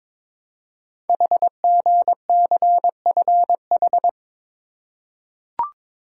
Morse code, heard as a keyed tone, about 22 wpm, 700 Hz. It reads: HGCFH
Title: Code Groups - 5 Letters - 22wpm